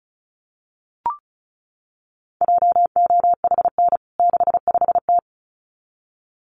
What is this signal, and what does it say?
Morse code, heard as a keyed tone, about 34 wpm, 700 Hz. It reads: JOHN 65T
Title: CWT Exchange 35wpm